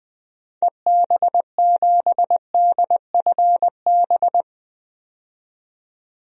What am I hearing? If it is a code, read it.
Morse code, heard as a keyed tone, about 20 wpm, 700 Hz. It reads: EB7DFB
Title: DXCC-100 Call Signs - 6 Characters - 20wpm